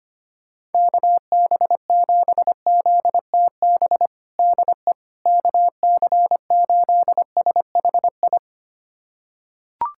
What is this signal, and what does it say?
Morse code, heard as a keyed tone, about 25 wpm, 700 Hz. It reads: KB7ZTB DE KC8SHI
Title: USA Callsign DE Callsign - 6 Characters - 25wpm